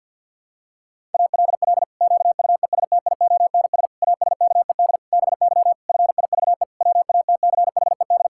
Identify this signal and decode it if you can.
Morse code, heard as a keyed tone, about 50 wpm, 700 Hz. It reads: ALL QUESTIONS ASKED BY FIVE WATCHED
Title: Pangram Challenge v2 - 50wpm